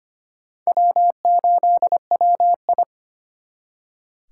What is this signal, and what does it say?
Morse code, heard as a keyed tone, about 32 wpm, 700 Hz. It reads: W8WI